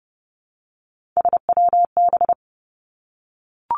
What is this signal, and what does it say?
Morse code, heard as a keyed tone, about 30 wpm, 700 Hz. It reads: SWB
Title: Three Challenging Letter-Number Combinations 30wpm